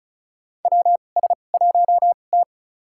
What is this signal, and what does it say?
Morse code, heard as a keyed tone, about 36 wpm, spaced as for 25 wpm, 700 Hz. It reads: WS1T
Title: USA Call Signs - 2x Character Spacing - 35wpm